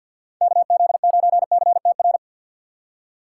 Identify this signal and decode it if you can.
Morse code, heard as a keyed tone, about 52 wpm, 700 Hz. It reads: KZ9CTR
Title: USA Call Signs - 6 Characters - 3x Character Spacing - 50wpm